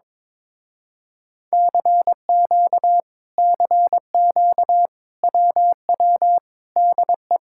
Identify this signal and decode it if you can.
Morse code, heard as a keyed tone, about 22 wpm, 700 Hz. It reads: CQ CQ WW DE
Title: Speed-Racing Common QSO Phrases - 22wpm